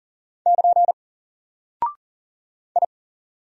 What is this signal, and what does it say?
Morse code, heard as a keyed tone, about 39 wpm, 700 Hz. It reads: <KN> I
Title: Single Character (Letter:Number:Punctuation:Prosign) Alt 40wpm